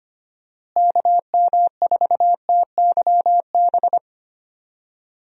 Morse code, heard as a keyed tone, about 25 wpm, 700 Hz. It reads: KM4TYB
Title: USA Call Signs - 6 Characters - Repeated 2x - 25wpm